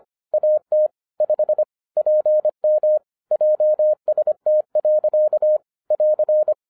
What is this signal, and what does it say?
Morse code, heard as a keyed tone, about 25 wpm, 610 Hz. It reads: AT 5 PM JST. <AR>